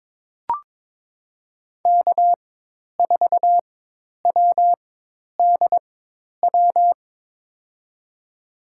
Morse code, heard as a keyed tone, about 21 wpm, spaced as for 10 wpm, 700 Hz. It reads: K4WDW
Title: USA Call Signs - 5 Characters - 4x Character Spacing - 22wpm